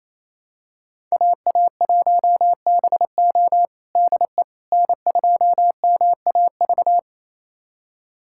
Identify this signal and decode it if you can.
Morse code, heard as a keyed tone, about 28 wpm, 700 Hz. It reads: AA1BO DE N2MAV
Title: USA Callsign DE Callsign - 5 Characters - 28wpm